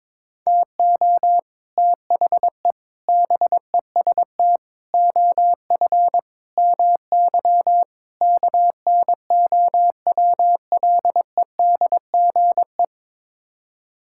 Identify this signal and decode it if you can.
Morse code, heard as a keyed tone, about 22 wpm, 700 Hz. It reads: TO THE BEST OF MY KNOWLEDGE
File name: practice / 22wpm / Sets of 6 Words - 22wpm